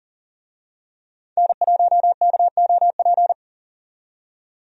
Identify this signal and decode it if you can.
Morse code, heard as a keyed tone, about 40 wpm, 700 Hz. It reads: N1KOP
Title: USA Call Signs - 5 Characters - 40wpm